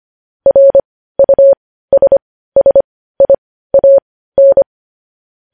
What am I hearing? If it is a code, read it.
Morse code, heard as a keyed tone, about 25 wpm, spaced as for 15 wpm, 550 Hz. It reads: RUSSIAN
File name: podcast / News headlines-1x-25x15